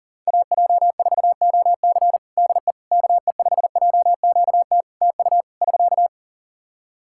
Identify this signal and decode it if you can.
Morse code, heard as a keyed tone, about 40 wpm, 700 Hz. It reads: AJ4OC DE KE5JQT TU <SK>
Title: K1USN SST QSO Practice 40wpm